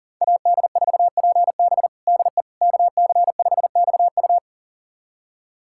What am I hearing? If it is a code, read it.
Morse code, heard as a keyed tone, about 37 wpm, 690 Hz. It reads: AD4PB DE KC5XU
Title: USA Callsign DE Callsign - 5 Characters - 40wpm